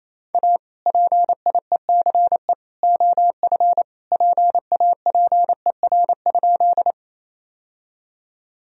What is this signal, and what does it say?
Morse code, heard as a keyed tone, about 28 wpm, 700 Hz. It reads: A PIECE OF PAPER?
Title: Sentences from Top 1000 Words - Review Part 2 - 28wpm